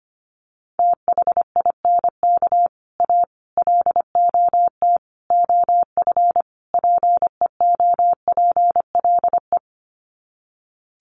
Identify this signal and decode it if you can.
Morse code, heard as a keyed tone, about 25 wpm, 700 Hz. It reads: THINK A LOT OF PEOPLE
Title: Sets of 5 Words 25wpm